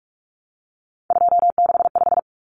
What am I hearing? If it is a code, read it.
Morse code, heard as a keyed tone, about 44 wpm, 700 Hz. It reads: 265
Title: Numbers - 3-Digit 45wpm